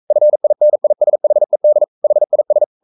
Morse code, heard as a keyed tone, about 42 wpm, 600 Hz. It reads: FINISHED HIS
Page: Romanen Never af Ken Follet på engelsk.